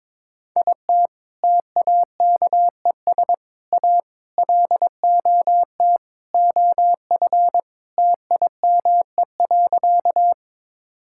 Morse code, heard as a keyed tone, about 22 wpm, 700 Hz. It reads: IT TAKES A LOT OF TIME.
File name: practice / 22wpm / Sentences with 4-Word Phrases - 22wpm